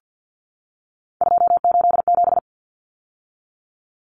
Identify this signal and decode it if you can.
Morse code, heard as a keyed tone, about 50 wpm, 700 Hz. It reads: 287
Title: Numbers - 3-Digit 50wpm